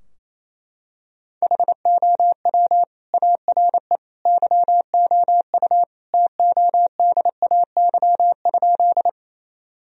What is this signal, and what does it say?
Morse code, heard as a keyed tone, about 28 wpm, 700 Hz. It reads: HOW ARE YOU TODAY?
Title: Sentences from Top 1000 Words - Review Part 4 - 28wpm